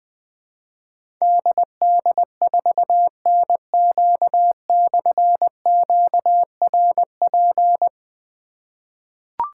Morse code, heard as a keyed tone, about 20 wpm, 700 Hz. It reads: DD4NQ/QRP
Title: Difficult Call Signs 20wpm